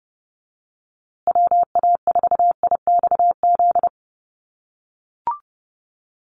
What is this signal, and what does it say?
Morse code, heard as a keyed tone, about 30 wpm, 700 Hz. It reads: WA4IXZ
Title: USA Call Signs Repeated 3x 30wpm